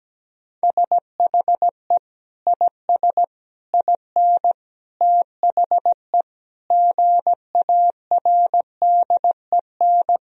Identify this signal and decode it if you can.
Morse code, heard as a keyed tone, about 17 wpm, 700 Hz. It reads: SHE IS IN THE GARDEN
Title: Sentences from Top 1000 Words - Review Part 4 - 17wpm